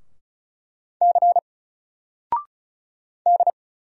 Morse code, heard as a keyed tone, about 36 wpm, 700 Hz. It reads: C D
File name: Single-Letter-Number-35wpm